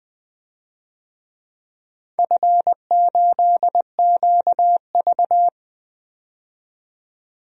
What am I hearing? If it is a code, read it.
Morse code, heard as a keyed tone, about 20 wpm, 700 Hz. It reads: F8QV
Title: DXCC-100 Call Signs - 4 Characters - 20wpm